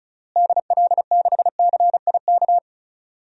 Morse code, heard as a keyed tone, about 34 wpm, 700 Hz. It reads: DL6CIK
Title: DXCC-100 Call Signs - 6 Characters - Repeated 3x - 35wpm